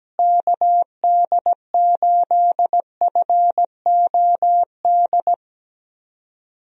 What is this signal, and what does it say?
Morse code, heard as a keyed tone, about 17 wpm, 700 Hz. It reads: KD8FOD
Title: USA Call Signs - 6 Characters - Repeated 2x - 17wpm